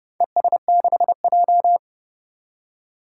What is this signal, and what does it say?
Morse code, heard as a keyed tone, about 30 wpm, 700 Hz. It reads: ES6J
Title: Four Challenging Letter-Number Combinations 30wpm